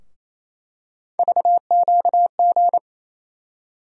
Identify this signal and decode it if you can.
Morse code, heard as a keyed tone, about 27 wpm, 700 Hz. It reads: VQG
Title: USA Call Sign Postfixes - 3 Letters 28wpm